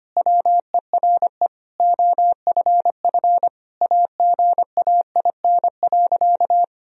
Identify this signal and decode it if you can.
Morse code, heard as a keyed tone, about 25 wpm, 700 Hz. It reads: WERE OFF AGAIN.
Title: Sentences from Top 1000 Words - Review Part 2 - 25wpm